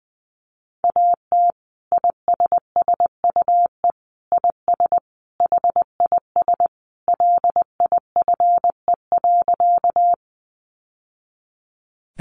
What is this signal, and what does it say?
Morse code, heard as a keyed tone, about 19 wpm, 700 Hz. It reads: AT ISSUE IS HIS LIFE.
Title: Sentences from Top 900 Words 20wpm